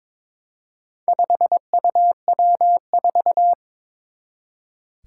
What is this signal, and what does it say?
Morse code, heard as a keyed tone, about 22 wpm, 700 Hz. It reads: 5UW4
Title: Four Challenging Letter-Number Combinations 22wpm